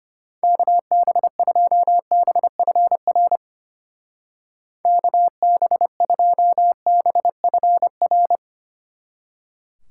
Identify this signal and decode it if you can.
Morse code, heard as a keyed tone, about 31 wpm, 700 Hz. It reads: KB2BFR KB2BFR
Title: Speed Racing - USA 2x3 Call Signs 25wpm